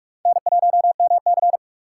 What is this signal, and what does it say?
Morse code, heard as a keyed tone, about 45 wpm, 700 Hz. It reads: N1MC
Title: USA Call Signs - 4 Characters - 8x Character Spacing - 45wpm